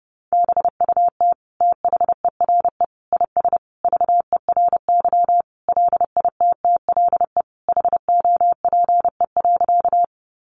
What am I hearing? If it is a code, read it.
Morse code, heard as a keyed tone, about 30 wpm, 700 Hz. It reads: BUT THERE IS VERY LITTLE HOPE.
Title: Sentences from Top 1000 Words 30wpm